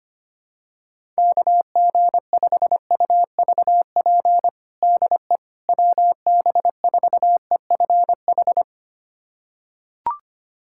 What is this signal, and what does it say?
Morse code, heard as a keyed tone, about 25 wpm, 700 Hz. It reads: KG5UVP DE WB4EFH
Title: USA Callsign DE Callsign 25wpm